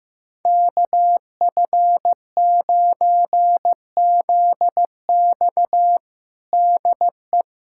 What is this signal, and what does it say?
Morse code, heard as a keyed tone, about 15 wpm, 700 Hz. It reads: KF9ZX DE
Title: USA Callsign DE Callsign - 5 Characters - 15wpm